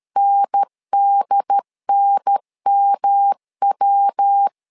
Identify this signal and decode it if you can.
Morse code, heard as a keyed tone, about 13 wpm, 800 Hz. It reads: NDNMW
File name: LEZIONI / 3 MWRO / 3 MW 12wpm 1633